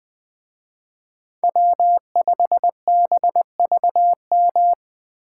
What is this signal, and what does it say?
Morse code, heard as a keyed tone, about 20 wpm, 700 Hz. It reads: W5BVM
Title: Speed Racing - USA 1x3 Call Signs 17wpm